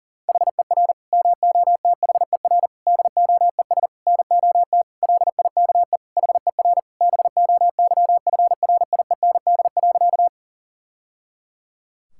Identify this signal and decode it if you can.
Morse code, heard as a keyed tone, about 40 wpm, 700 Hz. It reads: HER MOTHER DOES NOT LIKE HER BOYFRIEND.
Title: Sentences from Top 1000 Words - Review Part 5 - 40wpm